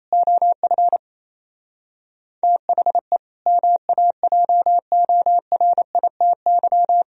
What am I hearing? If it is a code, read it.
Morse code, heard as a keyed tone, about 34 wpm, 700 Hz. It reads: OF THE MAJORITY
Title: Speed-Racing - Sets of 3 Words 28wpm